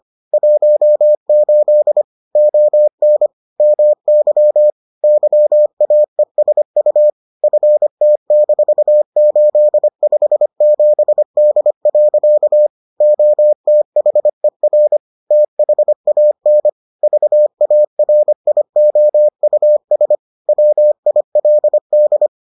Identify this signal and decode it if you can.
Morse code, heard as a keyed tone, about 25 wpm, 600 Hz. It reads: 18 ON MY YAESU FT-857D. OTHER THAN VARIOUS WILD